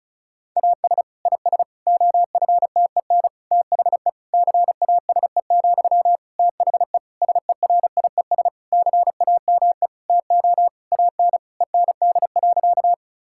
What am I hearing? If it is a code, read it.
Morse code, heard as a keyed tone, about 36 wpm, 700 Hz. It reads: AS IS OFTEN THE CASE, THE SERIES CAME TO AN END.
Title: Sentences from Top 1000 Words - Review Part 5 - 35wpm